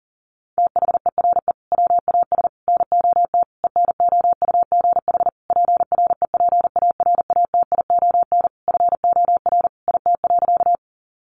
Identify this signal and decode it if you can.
Morse code, heard as a keyed tone, about 40 wpm, 700 Hz. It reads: THERE WAS NOT ENOUGH PREPARATION FOR IT.
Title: Sentences from Top 1400 Words 40wpm